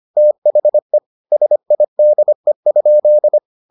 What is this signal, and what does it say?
Morse code, heard as a keyed tone, about 25 wpm, 600 Hz. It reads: THE SIDE?